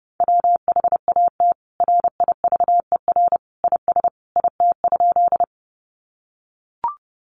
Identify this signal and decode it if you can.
Morse code, heard as a keyed tone, about 28 wpm, 700 Hz. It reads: WHAT RIVER IS IT?
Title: Sentences from Top 300 Words 30wpm